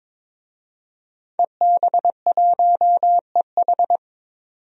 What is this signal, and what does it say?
Morse code, heard as a keyed tone, about 21 wpm, 700 Hz. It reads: EB1EH
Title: DXCC-100 Call Signs - 5 Characters - 22wpm